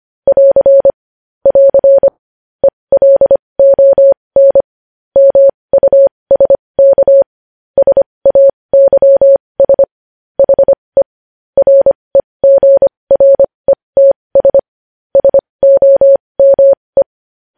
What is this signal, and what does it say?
Morse code, heard as a keyed tone, about 25 wpm, spaced as for 20 wpm, 550 Hz. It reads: <AR> <AR> ELON MUSK SAYS HE REGRETS SOME
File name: podcast / News headlines-1x-25x20